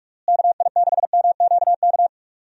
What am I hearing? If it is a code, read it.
Morse code, heard as a keyed tone, about 41 wpm, 700 Hz. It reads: KI6MQK
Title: USA Call Signs - 6 Characters - Repeated 3x - 45wpm